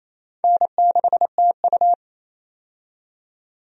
Morse code, heard as a keyed tone, about 28 wpm, 700 Hz. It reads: N6TU
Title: USA Call Signs - 4 Characters - 28wpm